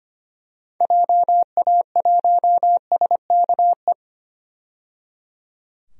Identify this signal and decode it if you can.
Morse code, heard as a keyed tone, about 25 wpm, 700 Hz. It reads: JA1SKE
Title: DXCC-100 Call Signs - 6 Characters - 25wpm